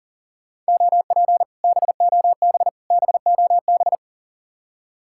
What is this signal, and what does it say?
Morse code, heard as a keyed tone, about 50 wpm, 700 Hz. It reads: OP BOB BOB
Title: Speed-Racing Common QSO Phrases - 40wpm